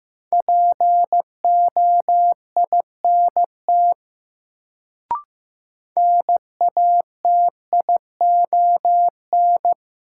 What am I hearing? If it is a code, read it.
Morse code, heard as a keyed tone, about 15 wpm, 700 Hz. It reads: POINT NATION
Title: Sets of 3 Words - New Word Drill - 15wpm